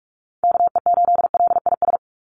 Morse code, heard as a keyed tone, about 44 wpm, 700 Hz. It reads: KE8LIS
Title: USA Call Signs 45wpm